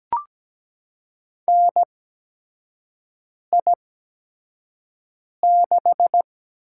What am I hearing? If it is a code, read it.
Morse code, heard as a keyed tone, about 17 wpm, spaced as for 5 wpm, 700 Hz. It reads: NI6
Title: USA Call Signs - 4 Characters - 8x Character Spacing - 17wpm